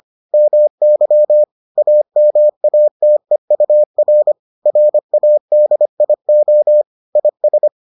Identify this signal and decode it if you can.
Morse code, heard as a keyed tone, about 25 wpm, 600 Hz. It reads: MY AMATEUR RADIO IS